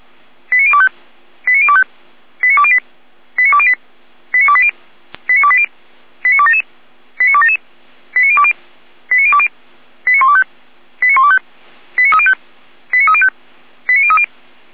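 Five-tone selective call (ZVEI), heard as quick runs of five tones, tones of about 70 ms. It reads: 89025 89025 89208 89208 89280 89280 89277 89277 89020 89020 89125 89125 89305 89305 89030
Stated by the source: Received on an SDR; ZVEI five-tone selective call.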